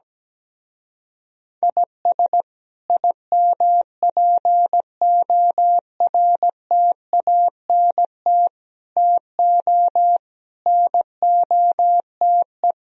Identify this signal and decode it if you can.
Morse code, heard as a keyed tone, about 17 wpm, 700 Hz. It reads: IS IMPORTANT TO NOTE
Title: Speed-Racing - Sets of 5 Words 17wpm